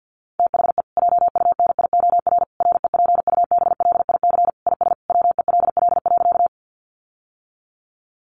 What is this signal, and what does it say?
Morse code, heard as a keyed tone, about 48 wpm, 700 Hz. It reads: THE JUNIOR REPUBLIC IS WELL.
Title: Sentences from Top 1500 Words 50wpm